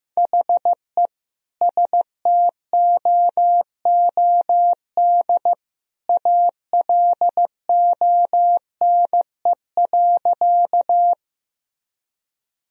Morse code, heard as a keyed tone, about 15 wpm, 700 Hz. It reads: HE STOOD ALONE.
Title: Sentences from Top 1000 Words - Review Part 2 - 15wpm